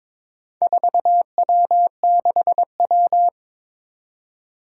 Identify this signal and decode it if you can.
Morse code, heard as a keyed tone, about 22 wpm, 700 Hz. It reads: 4W6W
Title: Four Challenging Letter-Number Combinations 22wpm